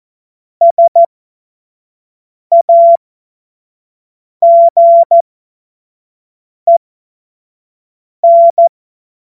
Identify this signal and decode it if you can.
Morse code, heard as a keyed tone, about 14 wpm, spaced as for 5 wpm, 680 Hz. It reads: SAGEN
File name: Morsekurs / Witze / Witz 02 April 2024 25 F70